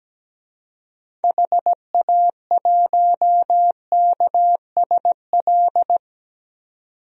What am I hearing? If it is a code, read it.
Morse code, heard as a keyed tone, about 17 wpm, 700 Hz. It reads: HA1KSL
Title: DXCC-100 Call Signs - 6 Characters - 17wpm